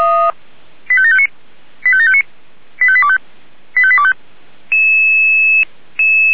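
Five-tone selective call (ZVEI), heard as quick runs of five tones, tones of about 70 ms. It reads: 85759 85759 85825 85825
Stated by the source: ZVEI five-tone selective call 85730 received by SDR.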